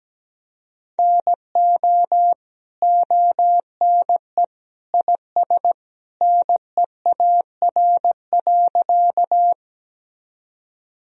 Morse code, heard as a keyed tone, about 17 wpm, 700 Hz. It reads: NO ONE IS NEAR.
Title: Sentences from Top 1000 Words - Review Part 1 - 17wpm